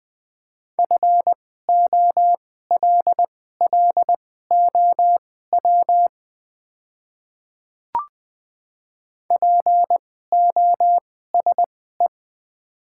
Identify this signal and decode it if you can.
Morse code, heard as a keyed tone, about 20 wpm, spaced as for 15 wpm, 690 Hz. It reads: FOLLOW POSE
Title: 2x Character Spacing - Top 500 Words - 20wpm